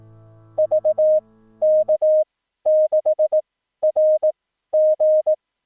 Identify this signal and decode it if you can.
Morse code, heard as a keyed tone, about 18 wpm, spaced as for 12 wpm, 620 Hz. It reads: VK6RG
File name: newswest